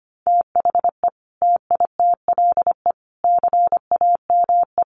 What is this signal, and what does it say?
Morse code, heard as a keyed tone, about 25 wpm, 700 Hz. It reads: THE TITLE CAME
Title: Sentences from Top 1200 Words 25wpm